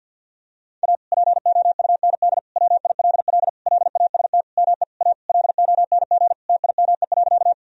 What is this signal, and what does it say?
Morse code, heard as a keyed tone, about 50 wpm, 700 Hz. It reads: A POUND WILL LAST ME A LONG TIME.
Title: Sentences from Top 1000 Words - Review Part 2 - 50wpm